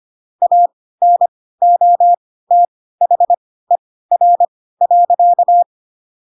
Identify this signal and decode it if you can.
Morse code, heard as a keyed tone, about 25 wpm, spaced as for 16 wpm, 700 Hz. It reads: ANOTHER.
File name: Pinocchio / Pinocchio ch3 10 25 ik1hge